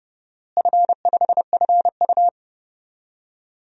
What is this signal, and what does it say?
Morse code, heard as a keyed tone, about 31 wpm, 700 Hz. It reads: F5FU
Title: DXCC-100 Call Signs - 4 Characters - Repeated 2x - 30wpm